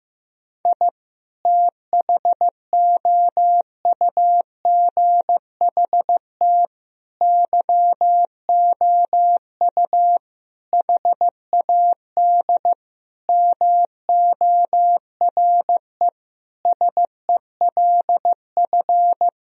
Morse code, heard as a keyed tone, about 15 wpm, 700 Hz. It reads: I THOUGHT YOU HAD MORE SELF
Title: Sentences from Top 1000 Words - Review Part 3 - 15wpm